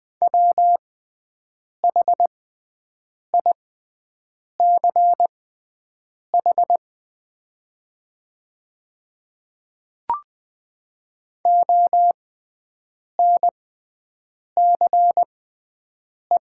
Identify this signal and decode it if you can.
Morse code, heard as a keyed tone, about 20 wpm, spaced as for 7 wpm, 700 Hz. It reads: WHICH ONCE
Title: 6x Character Spacing - Top 300 Words - 20wpm